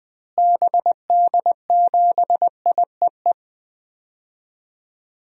Morse code, heard as a keyed tone, about 21 wpm, 700 Hz. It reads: BD7IEE
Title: DXCC-100 Call Signs - 6 Characters - Repeated 2x - 20wpm